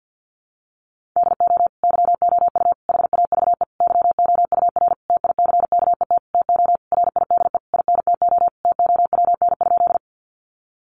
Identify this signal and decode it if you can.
Morse code, heard as a keyed tone, about 50 wpm, 700 Hz. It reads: DO YOU HAVE YOUR TICKET TO RIDE INTO TOWN?
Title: Sentences from Top 1400 Words 50wpm